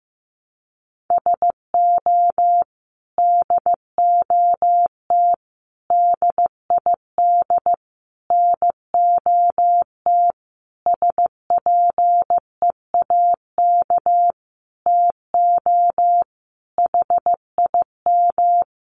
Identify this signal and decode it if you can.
Morse code, heard as a keyed tone, about 15 wpm, 700 Hz. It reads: SO DOT DID NOT SPEAK TO HIM
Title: Sentences from Top 1500 Words 15wpm